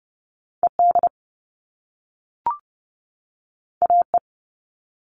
Morse code, heard as a keyed tone, about 30 wpm, 700 Hz. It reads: E D A E
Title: All Cut Numbers - 2-Digit 30wpm